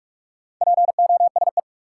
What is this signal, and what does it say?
Morse code, heard as a keyed tone, about 47 wpm, 700 Hz. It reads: POSE
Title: Sentences from Top 1000 Words - Review Part 2 - New Word Drill - 45wpm